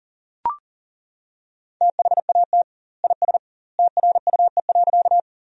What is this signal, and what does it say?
Morse code, heard as a keyed tone, about 39 wpm, 700 Hz. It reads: THAT IS TRUE.
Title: 2x Word-Spacing - Sentences from Top 400 Words - 40wpm